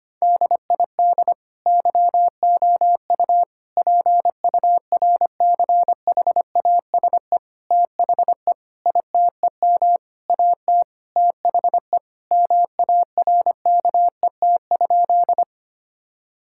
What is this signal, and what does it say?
Morse code, heard as a keyed tone, about 25 wpm, 700 Hz. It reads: DID YOU PURCHASE THE ITEM AT THE MARKET?
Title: Sentences from Top 1000 Words - Review Part 6 - 25wpm